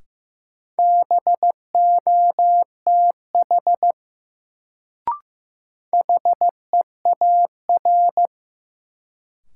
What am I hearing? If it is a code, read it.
Morse code, heard as a keyed tone, about 15 wpm, 700 Hz. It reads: BOTH HEAR
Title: Sentences from Top 300 Words - New Word Drill - 15wpm